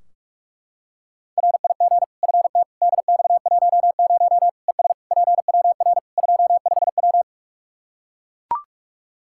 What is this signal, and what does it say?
Morse code, heard as a keyed tone, about 41 wpm, 700 Hz. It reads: RIG FT DX10 ES PWR 25W
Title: Practice QSO - Exchange 2 - Info - 45wpm